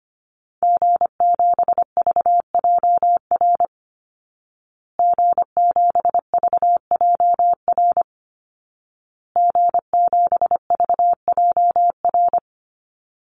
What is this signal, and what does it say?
Morse code, heard as a keyed tone, about 25 wpm, 700 Hz. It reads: G7VJR G7VJR G7VJR
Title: Call Signs Repeated 2x - 25wpm